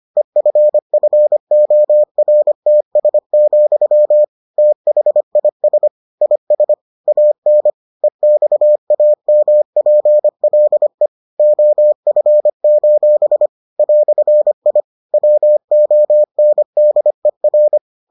Morse code, heard as a keyed tone, about 25 wpm, 600 Hz. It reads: EFFORTS, THIS IS AN EXAMPLE OF: "I WONDER